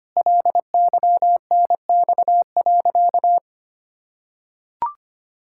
Morse code, heard as a keyed tone, about 25 wpm, 700 Hz. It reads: LYNX.
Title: Pangram Challenge v2 - 25wpm